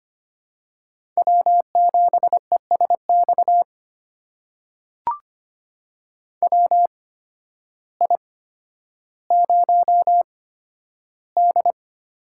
Morse code, heard as a keyed tone, about 25 wpm, spaced as for 12 wpm, 700 Hz. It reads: W7ESX WI0D
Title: USA Call Signs - 8x Character Spacing - 25wpm